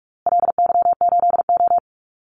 Morse code, heard as a keyed tone, about 44 wpm, 700 Hz. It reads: LY8O
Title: Call Signs Repeated 2x - 45wpm